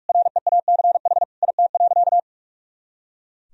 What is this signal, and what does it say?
Morse code, heard as a keyed tone, about 43 wpm, 700 Hz. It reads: REACH IT.
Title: Sentences from Top 400 Words and Abbreviations 45wpm